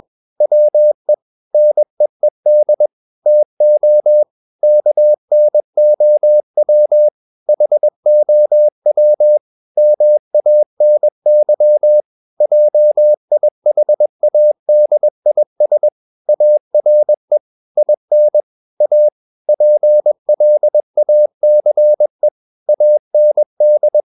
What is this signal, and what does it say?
Morse code, heard as a keyed tone, about 21 wpm, 600 Hz. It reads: WE NEED TO KNOW HOW MANY JIHADIS ARE IN A PLACE AND
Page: CW med Gnister